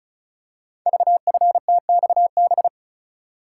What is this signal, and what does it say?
Morse code, heard as a keyed tone, about 36 wpm, 690 Hz. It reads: VFTXB
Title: Code Groups - 5 Letters - 35wpm